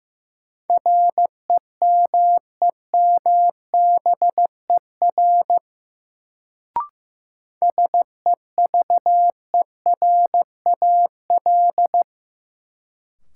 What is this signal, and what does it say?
Morse code, heard as a keyed tone, about 16 wpm, 700 Hz. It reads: REMEMBER SEVERAL
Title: Sentences from Top 400 Words - New Word Drill - 15wpm